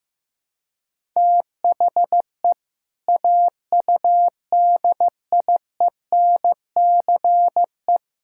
Morse code, heard as a keyed tone, about 15 wpm, 700 Hz. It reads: THE AUDIENCE
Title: Sentences from Top 1000 Words - Review Part 5 - 15wpm